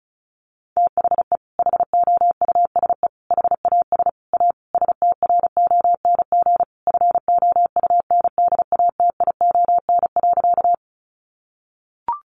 Morse code, heard as a keyed tone, about 34 wpm, 700 Hz. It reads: THE HOUSE HAS A STRONG FOUNDATION.
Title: Sentences from Top 1200 Words 35wpm